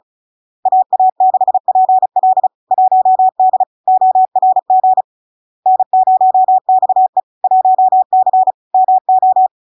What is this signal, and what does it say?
Morse code, heard as a keyed tone, about 36 wpm, 750 Hz. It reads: AA6PL 1D ORG N0XE 1C MO
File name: LARK-FD-2022-Log-in-CW-35wpm-750-Hz